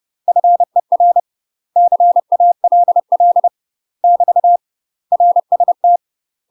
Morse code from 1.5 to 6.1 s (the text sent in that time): CALL <BT> RST